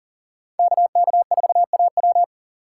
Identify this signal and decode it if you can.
Morse code, heard as a keyed tone, about 48 wpm, 690 Hz. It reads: KK4AW
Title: Speed Racing - USA Call Signs 40wpm